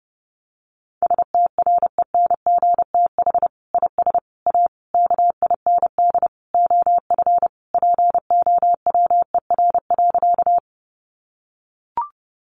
Morse code, heard as a keyed tone, about 30 wpm, 700 Hz. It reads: STRENGTH IS A KIND OF POWER.
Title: Sentences from Top 1300 Words 30wpm